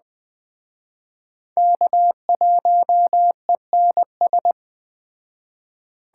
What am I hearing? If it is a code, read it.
Morse code, heard as a keyed tone, about 20 wpm, 700 Hz. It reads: K1ENS